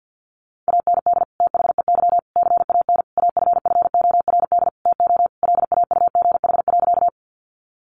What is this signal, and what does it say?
Morse code, heard as a keyed tone, about 49 wpm, 700 Hz. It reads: AND THEY CAN AFFORD TO LAUGH.
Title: Sentences from Top 1100 Words 50wpm